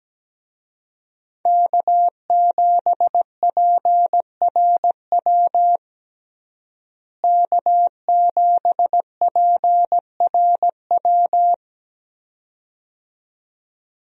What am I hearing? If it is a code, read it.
Morse code, heard as a keyed tone, about 17 wpm, 700 Hz. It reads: K7PRW K7PRW
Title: USA Call Signs - 5 Characters - Repeated 2x - 17wpm